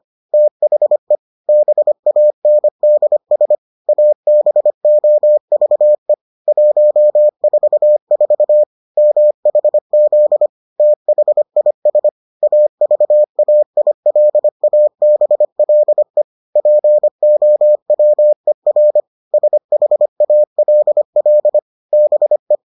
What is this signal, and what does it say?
Morse code, heard as a keyed tone, about 25 wpm, 600 Hz. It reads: THE BANDS ABOVE 144 MHZ THIS AVAILABLE POWER SHALL BE